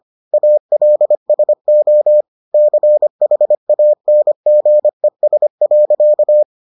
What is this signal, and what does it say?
Morse code, heard as a keyed tone, about 25 wpm, 600 Hz. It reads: ALSO CHANGES.